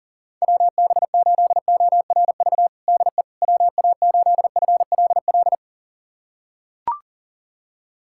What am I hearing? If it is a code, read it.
Morse code, heard as a keyed tone, about 40 wpm, 700 Hz. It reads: WB8ORV DE WA8FLL
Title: USA Callsign DE Callsign - 6 Characters - 40wpm